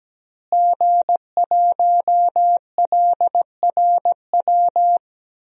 Morse code, heard as a keyed tone, about 17 wpm, 700 Hz. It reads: G1LRW
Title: DXCC-100 Call Signs - 5 Characters - Repeated 3x - 17wpm